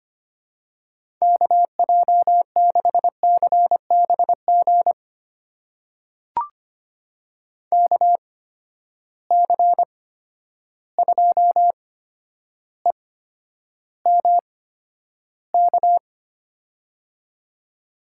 Morse code, heard as a keyed tone, about 25 wpm, spaced as for 11 wpm, 700 Hz. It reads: KJ6CBG KC2EMK
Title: USA Call Signs - 6 Characters - 8x Character Spacing - 25wpm